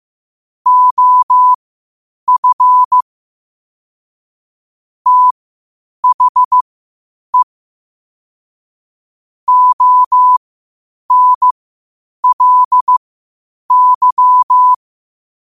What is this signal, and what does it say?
Morse code, heard as a keyed tone, about 15 wpm, spaced as for 8 wpm, 1000 Hz. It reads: OF THE ONLY
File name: qotdfiles / 8wpm